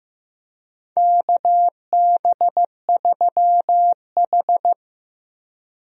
Morse code, heard as a keyed tone, about 16 wpm, 700 Hz. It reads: KB3H
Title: USA Call Signs - 4 Characters - Repeated 3x - 15wpm